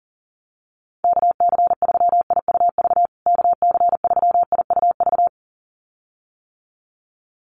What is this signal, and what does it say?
Morse code, heard as a keyed tone, about 40 wpm, 700 Hz. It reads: KC3IUV KC3IUV
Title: USA Call Signs Repeated 2x 40wpm